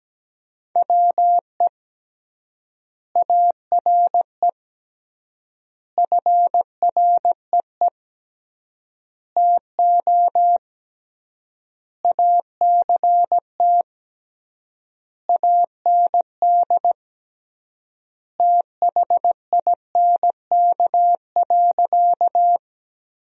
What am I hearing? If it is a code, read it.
Morse code, heard as a keyed tone, about 17 wpm, 700 Hz. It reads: WE ARE FREE TO ACT AND THINK.
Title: 3x Word-Spacing - Sentences from Top 500 Words - 17wpm